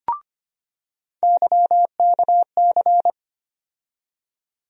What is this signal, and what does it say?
Morse code, heard as a keyed tone, about 25 wpm, 700 Hz. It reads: YKC
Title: USA Call Sign Postfixes - 3 Letters 25wpm